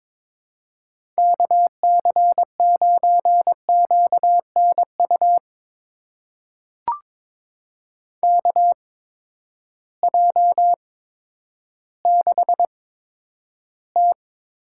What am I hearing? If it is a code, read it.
Morse code, heard as a keyed tone, about 22 wpm, spaced as for 11 wpm, 700 Hz. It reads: KC9QNU KJ6T
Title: USA Call Signs - 6 Characters - 8x Character Spacing - 22wpm